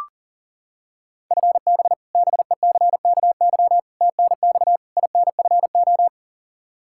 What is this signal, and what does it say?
Morse code, heard as a keyed tone, about 40 wpm, 700 Hz. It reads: FB BECKY TNX INFO
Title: Practice QSO - Exchange 3 - More Info - 40wpm